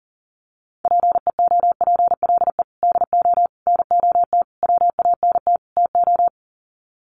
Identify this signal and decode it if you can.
Morse code, heard as a keyed tone, about 40 wpm, 700 Hz. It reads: PEOPLE DO NOT WANT TO
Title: Sets of 5 Words Encore2 40wpm